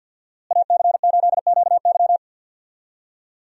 Morse code, heard as a keyed tone, about 50 wpm, 700 Hz. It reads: AC8QY
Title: USA Call Signs - 5 Characters - Repeated 2x - 50wpm